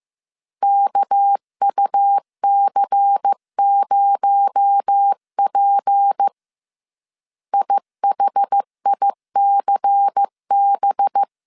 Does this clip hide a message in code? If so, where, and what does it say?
Yes: Morse code from 0.6 to 11.2 s, heard as a keyed tone, about 15 wpm, 800 Hz. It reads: KUC0P IHICB